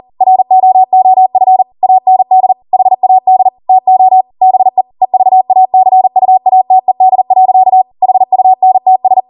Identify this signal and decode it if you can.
Morse code, heard as a keyed tone, about 40 wpm, 750 Hz. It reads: ROOF AND HAD TO BE EVACUATED. HUNTS